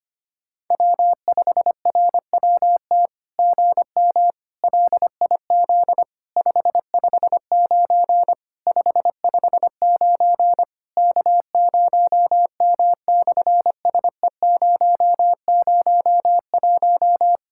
Morse code, heard as a keyed tone, about 25 wpm, 700 Hz. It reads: W5RWT GM LIZ 559 559 K0M/SE001
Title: SOTA QSO Practice 25wpm